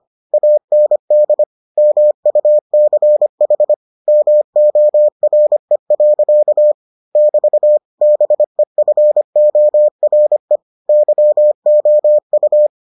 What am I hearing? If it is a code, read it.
Morse code, heard as a keyed tone, about 25 wpm, 600 Hz. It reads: AND MUCH MORE. <BT> BEFORE YOU